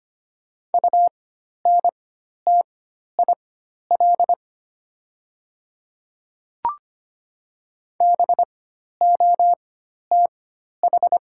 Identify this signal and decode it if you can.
Morse code, heard as a keyed tone, about 25 wpm, spaced as for 12 wpm, 690 Hz. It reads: UNTIL BOTH
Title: 4x Character Spacing - Top 300 Words - 25wpm